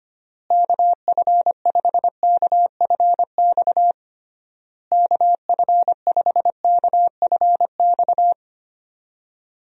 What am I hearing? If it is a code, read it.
Morse code, heard as a keyed tone, about 25 wpm, 700 Hz. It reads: KF5KFX KF5KFX
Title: USA Call Signs - 6 Characters - Repeated 3x - 25wpm